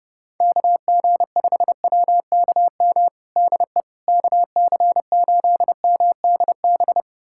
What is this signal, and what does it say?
Morse code, heard as a keyed tone, about 31 wpm, 700 Hz. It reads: KG5WKM DE KC8MDB
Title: USA Callsign DE Callsign 30wpm